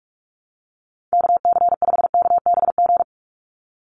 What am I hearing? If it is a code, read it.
Morse code, heard as a keyed tone, about 46 wpm, 690 Hz. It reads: KC5KBG
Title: USA Call Signs 45wpm